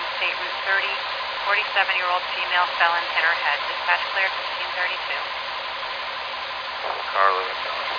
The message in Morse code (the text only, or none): none